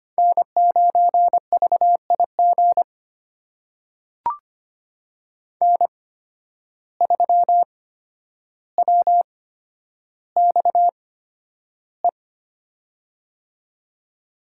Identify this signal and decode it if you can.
Morse code, heard as a keyed tone, about 25 wpm, spaced as for 11 wpm, 700 Hz. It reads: N9VIG N3WXE
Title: USA Call Signs - 5 Characters - 8x Character Spacing - 25wpm